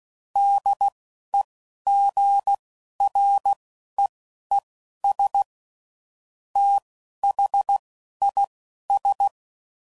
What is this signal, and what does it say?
Morse code, heard as a keyed tone, about 16 wpm, spaced as for 11 wpm, 800 Hz. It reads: DEGREES THIS
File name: CW Code Groups @10WPM / QSOs / QSO WD0FBT